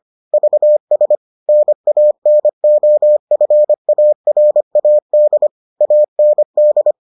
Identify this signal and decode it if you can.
Morse code, heard as a keyed tone, about 25 wpm, 600 Hz. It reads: VS NANOFARAD AND